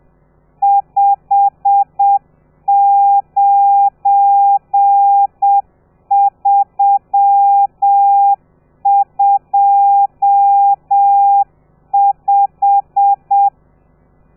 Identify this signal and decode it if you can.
Morse code, heard as a keyed tone, about 7 wpm, 800 Hz. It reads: 59325
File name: telegrafia / i2tfj / Lez12 30